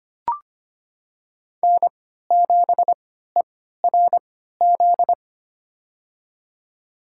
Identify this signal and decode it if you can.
Morse code, heard as a keyed tone, about 25 wpm, spaced as for 14 wpm, 700 Hz. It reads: N7ERZ
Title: USA Call Signs - 5 Characters - 3x Character Spacing - 25wpm